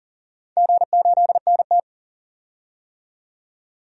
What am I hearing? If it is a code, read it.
Morse code, heard as a keyed tone, about 40 wpm, 700 Hz. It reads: G8NT
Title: DXCC-100 Call Signs - 4 Characters - 40wpm